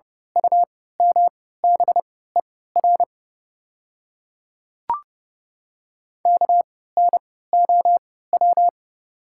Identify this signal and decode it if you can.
Morse code, heard as a keyed tone, about 30 wpm, spaced as for 17 wpm, 690 Hz. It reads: UMBER KNOW
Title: 3x Character Spacing - Top 200 Words - 30wpm